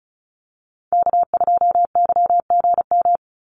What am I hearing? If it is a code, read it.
Morse code, heard as a keyed tone, about 34 wpm, 700 Hz. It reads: K2YGM
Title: Call Signs - 35wpm